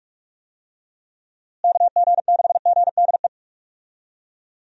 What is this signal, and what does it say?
Morse code, heard as a keyed tone, about 44 wpm, 700 Hz. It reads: KG6GDE
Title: USA Call Signs - 6 Characters - Repeated 2x - 45wpm